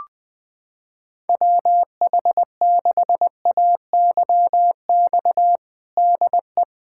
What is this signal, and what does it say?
Morse code, heard as a keyed tone, about 21 wpm, 700 Hz. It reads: WH6AYX DE
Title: USA Callsign DE Callsign 20wpm